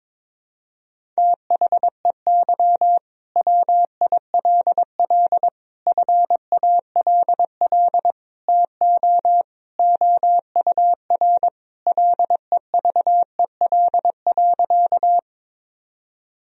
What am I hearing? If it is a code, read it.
Morse code, heard as a keyed tone, about 22 wpm, 700 Hz. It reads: THEY WILL FALL TO OUR LEVEL.
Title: Sentences from Top 1000 Words - Review Part 4 - 22wpm